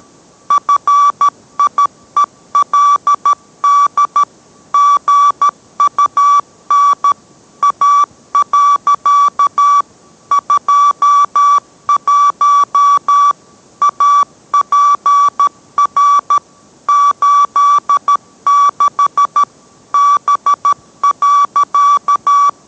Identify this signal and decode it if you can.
Morse code, heard as a keyed tone, about 13 wpm, 1150 Hz. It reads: FIELD GUN A. 21 APR 86 B.